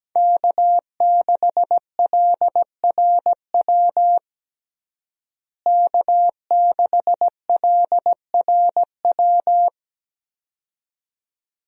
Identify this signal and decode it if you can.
Morse code, heard as a keyed tone, about 17 wpm, 700 Hz. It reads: K6LRW K6LRW
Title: USA Call Signs - 5 Characters - Repeated 2x - 17wpm